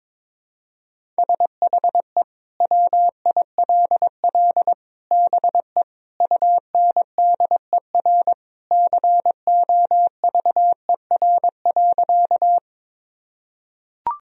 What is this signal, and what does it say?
Morse code, heard as a keyed tone, about 22 wpm, 700 Hz. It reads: SHE WILL BE UNDER COVER.
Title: Sentences from Top 1000 Words - Review Part 1 - 22wpm